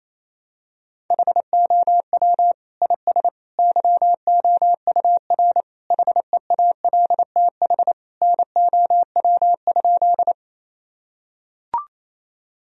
Morse code, heard as a keyed tone, about 27 wpm, 700 Hz. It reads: HOW IS YOUR HEALTH NOW?
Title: Sentences from Top 1000 Words - Review Part 3 - 28wpm